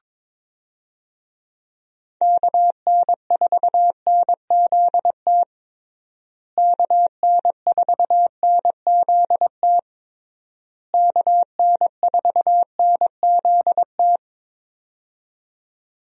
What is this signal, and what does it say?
Morse code, heard as a keyed tone, about 21 wpm, 700 Hz. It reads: KN4NZT KN4NZT KN4NZT
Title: USA Call Signs - 6 Characters - Repeated 3x - 22wpm